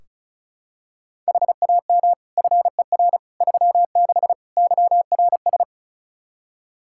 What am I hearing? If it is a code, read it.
Morse code, heard as a keyed tone, about 34 wpm, 700 Hz. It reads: HAM FER 36 YRS
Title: K1USN SST QSO Practice 35wpm